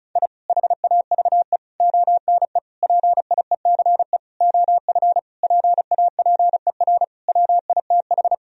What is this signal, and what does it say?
Morse code, heard as a keyed tone, about 34 wpm, 700 Hz. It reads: I HAVE ONE PIECE OF PAPER WITH
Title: Sentences from Top 1000 Words - Review Part 2 - 35wpm